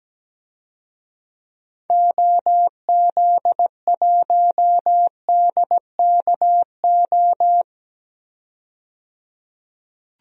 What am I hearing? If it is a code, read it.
Morse code, heard as a keyed tone, about 17 wpm, 700 Hz. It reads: OZ1DKO
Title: DXCC-100 Call Signs - 6 Characters - 17wpm